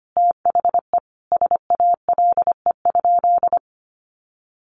Morse code, heard as a keyed tone, about 25 wpm, 700 Hz. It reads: THE SALE?
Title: Sentences from Top 1400 Words 25wpm